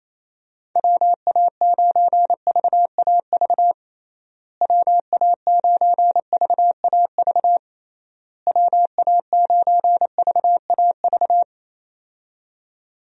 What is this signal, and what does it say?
Morse code, heard as a keyed tone, about 27 wpm, 700 Hz. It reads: WA9VAV WA9VAV WA9VAV
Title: USA Call Signs - 6 Characters - Repeated 3x - 28wpm